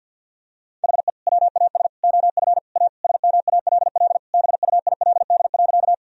Morse code, heard as a keyed tone, about 48 wpm, 700 Hz. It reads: HE WAS OF A SMALL BUILD.
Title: Sentences from Top 1000 Words - Review Part 1 - 50wpm